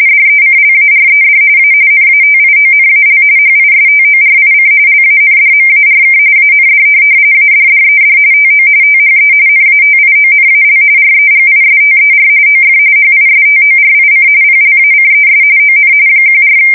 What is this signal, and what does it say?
RTTY, about 45.45 baud. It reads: DECLARED A NATIONAL HISTORIC LANDMARK AND ADDED TO THE NATIONAL REGISTER OF HISTORIC PLACES ON 14
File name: fsk170 / USS-Massachusetts